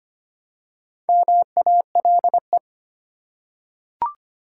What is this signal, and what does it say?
Morse code, heard as a keyed tone, about 25 wpm, 700 Hz. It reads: MALE
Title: Four Letter Words - 25wpm